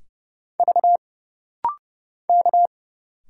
Morse code, heard as a keyed tone, about 30 wpm, 700 Hz. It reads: V K
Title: Single Letters - Rapid-Fire 30wpm